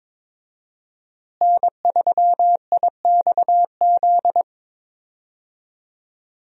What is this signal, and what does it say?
Morse code, heard as a keyed tone, about 22 wpm, 700 Hz. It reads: N3IXZ
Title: USA Call Signs - 5 Characters - Repeated 2x - 22wpm